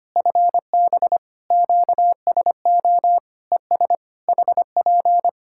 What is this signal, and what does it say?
Morse code, heard as a keyed tone, about 25 wpm, 700 Hz. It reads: FB QSO ES HP
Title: K1USN SST QSO Practice 25wpm